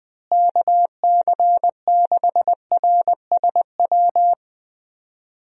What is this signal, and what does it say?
Morse code, heard as a keyed tone, about 20 wpm, 700 Hz. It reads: KC6RSW
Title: USA Call Signs - 6 Characters - Repeated 3x - 20wpm